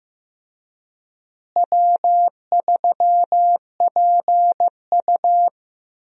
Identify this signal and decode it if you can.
Morse code, heard as a keyed tone, about 15 wpm, 700 Hz. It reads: W3PU
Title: USA Call Signs - 4 Characters - Repeated 2x - 15wpm